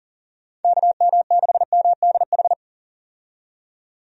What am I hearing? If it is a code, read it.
Morse code, heard as a keyed tone, about 40 wpm, 700 Hz. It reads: KM6MDH
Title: USA Call Signs - 6 Characters - 40wpm